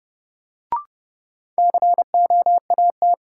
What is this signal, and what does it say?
Morse code, heard as a keyed tone, about 32 wpm, 700 Hz. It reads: COAT
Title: Sentences from Top 1600 Words - New Word Drill Alt - 30wpm